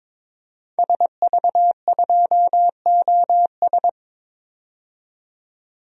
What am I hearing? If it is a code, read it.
Morse code, heard as a keyed tone, about 22 wpm, 700 Hz. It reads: SV2OS
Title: DXCC-100 Call Signs - 5 Characters - 22wpm